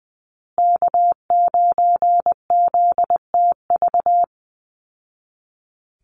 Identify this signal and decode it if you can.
Morse code, heard as a keyed tone, about 20 wpm, 700 Hz. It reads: K9ZTV
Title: Call Signs Repeated 2x - 20wpm